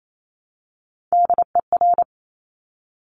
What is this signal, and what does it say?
Morse code, heard as a keyed tone, about 28 wpm, 700 Hz. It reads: DER
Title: Top Three Letter Combinations in Words 28wpm